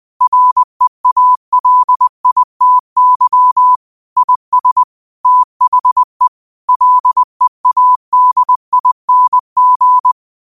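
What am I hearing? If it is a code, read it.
Morse code, heard as a keyed tone, about 20 wpm, 1000 Hz. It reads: REALITY IS THE LEADING
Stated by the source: 20 WPM Morse code quotes, QOTD at 20 WPM.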